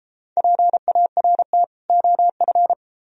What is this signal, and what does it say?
Morse code, heard as a keyed tone, about 31 wpm, 700 Hz. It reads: PART OF